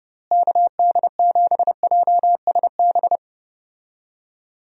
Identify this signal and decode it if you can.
Morse code, heard as a keyed tone, about 31 wpm, 700 Hz. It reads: KD7JSB
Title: USA Call Signs - 6 Characters - Repeated 3x - 30wpm